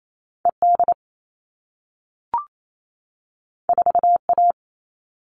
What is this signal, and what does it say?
Morse code, heard as a keyed tone, about 28 wpm, 700 Hz. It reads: ED 4A
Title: All Cut Numbers - 2-Digit 28wpm